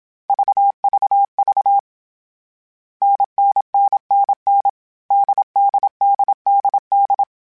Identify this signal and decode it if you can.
Morse code, heard as a keyed tone, about 27 wpm, 800 Hz. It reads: VVV NNNNN DDDDD
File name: Variable Speed / Variable Speed-mp3s / GZ GZ7 sp @25